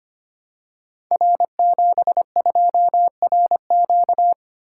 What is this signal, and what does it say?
Morse code, heard as a keyed tone, about 25 wpm, 700 Hz. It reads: R72RQ
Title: Code Groups - 5 Letter:Numbers - 25wpm